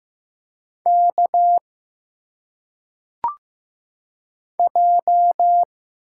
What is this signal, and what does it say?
Morse code, heard as a keyed tone, about 15 wpm, 700 Hz. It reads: K J
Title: Single Letters - Phonetic Pronunciation 15wpm